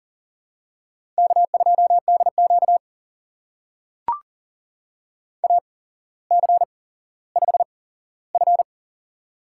Morse code, heard as a keyed tone, about 40 wpm, spaced as for 17 wpm, 700 Hz. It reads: K2DQ AC5F
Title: USA Call Signs - 4 Characters - 8x Character Spacing - 40wpm